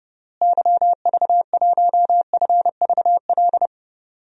Morse code, heard as a keyed tone, about 30 wpm, 700 Hz. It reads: YV1FVL
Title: DXCC-100 Call Signs - 6 Characters - Repeated 2x - 30wpm